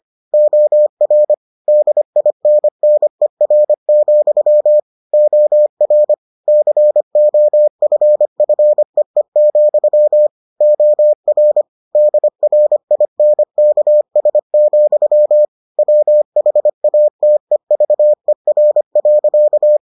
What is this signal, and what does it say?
Morse code, heard as a keyed tone, about 25 wpm, 600 Hz. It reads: OR DINNER, OR COFFEE, OR DRINKS, WHATEVER.